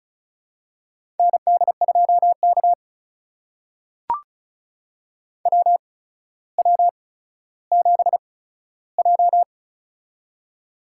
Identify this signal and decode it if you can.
Morse code, heard as a keyed tone, about 36 wpm, spaced as for 15 wpm, 700 Hz. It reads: ND2K WW7J
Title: USA Call Signs - 8x Character Spacing - 35wpm